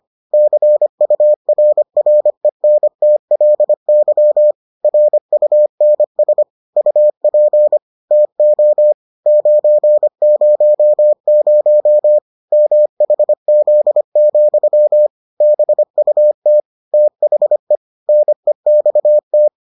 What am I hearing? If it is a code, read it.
Morse code, heard as a keyed tone, about 25 wpm, 600 Hz. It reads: CURRENTLY RUNS UP TO 900 MHZ, BUT THE NEXT